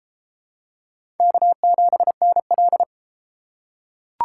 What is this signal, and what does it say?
Morse code, heard as a keyed tone, about 30 wpm, 700 Hz. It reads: K7NL